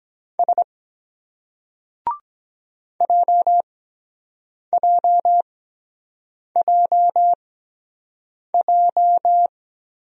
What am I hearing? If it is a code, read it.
Morse code, heard as a keyed tone, about 23 wpm, spaced as for 7 wpm, 700 Hz. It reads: S JJJJ